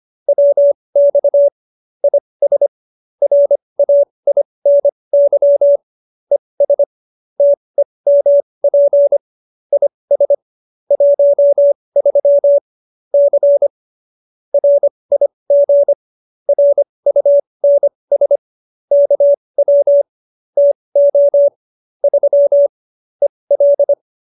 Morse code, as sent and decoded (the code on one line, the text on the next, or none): .-- -..- / .. ... / .-. .- .. -. -.-- / . ... / - . -- .--. / .. ... / .---- ...-- / -.-. / .-. .. --. / .-. ..- -. ... / -.- .-- / - --- / ...-- / . .-..
WX IS RAINY ES TEMP IS 13 C RIG RUNS KW TO 3 EL